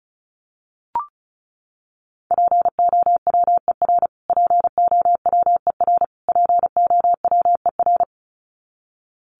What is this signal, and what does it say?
Morse code, heard as a keyed tone, about 34 wpm, 700 Hz. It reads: POWER POWER POWER
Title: Top 500 Words Repeated 35wpm